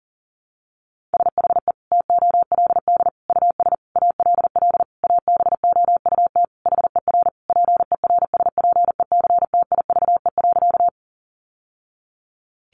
Morse code, heard as a keyed tone, about 40 wpm, 700 Hz. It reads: SHE TOLD US ALL ABOUT HER PERSPECTIVE.
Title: Sentences from Top 1200 Words 40wpm